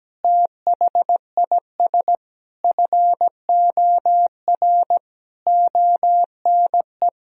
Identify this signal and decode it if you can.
Morse code, heard as a keyed tone, about 17 wpm, 700 Hz. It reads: THIS FOR ONE
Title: Sentences from Top 1000 Words - Review Part 1 - 17wpm